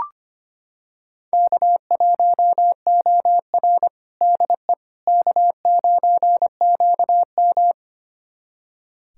Morse code, heard as a keyed tone, about 25 wpm, 700 Hz. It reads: K1OR DE K9QM
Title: USA Callsign DE Callsign - 4 Characters - 25wpm